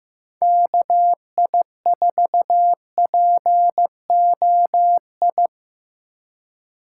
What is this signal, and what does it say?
Morse code, heard as a keyed tone, about 18 wpm, 700 Hz. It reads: KI4POI
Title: Speed Racing - USA Call Signs 15wpm